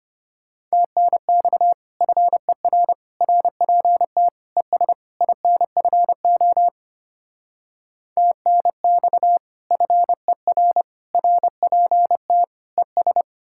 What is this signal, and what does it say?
Morse code, heard as a keyed tone, about 33 wpm, 700 Hz. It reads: TNX FER RPT ES INFO TNX FER RPT ES